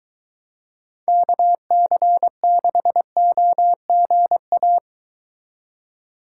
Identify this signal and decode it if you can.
Morse code, heard as a keyed tone, about 23 wpm, 700 Hz. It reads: KC6OGA